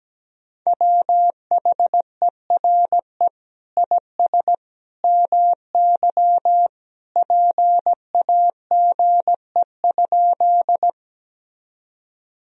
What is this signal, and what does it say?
Morse code, heard as a keyed tone, about 17 wpm, 700 Hz. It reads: WHERE IS MY PAGE?
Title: Sentences from Top 1000 Words - Review Part 1 - 17wpm